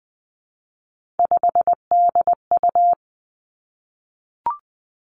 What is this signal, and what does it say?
Morse code, heard as a keyed tone, about 20 wpm, 700 Hz. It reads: 5DU
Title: Three Challenging Letter-Number Combinations 20wpm